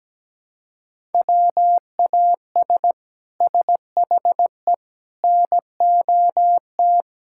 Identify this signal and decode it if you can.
Morse code, heard as a keyed tone, about 17 wpm, 700 Hz. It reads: WAS SHE NOT
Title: Sentences from Top 1000 Words - Review Part 5 - 17wpm